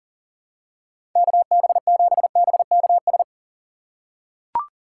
Morse code, heard as a keyed tone, about 40 wpm, 700 Hz. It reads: KB7BKS
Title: USA Call Signs - 6 Characters - 6x Character Spacing - 40wpm